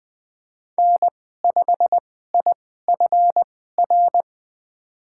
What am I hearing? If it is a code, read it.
Morse code, heard as a keyed tone, about 20 wpm, spaced as for 14 wpm, 700 Hz. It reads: N5IFR
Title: USA Call Signs - 2x Character Spacing - 20wpm